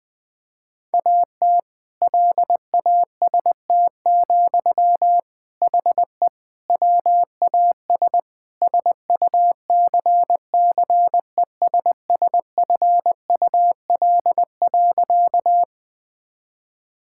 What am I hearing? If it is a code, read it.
Morse code, heard as a keyed tone, about 20 wpm, 700 Hz. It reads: AT LAST, HE WAS SUCCESSFUL.
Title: Sentences from Top 1000 Words - Review Part 6 - 20wpm